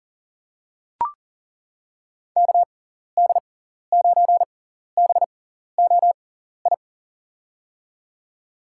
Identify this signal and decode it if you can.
Morse code, heard as a keyed tone, about 40 wpm, spaced as for 14 wpm, 700 Hz. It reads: KD9BOI
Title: USA Call Signs - 6x Character Spacing - 40wpm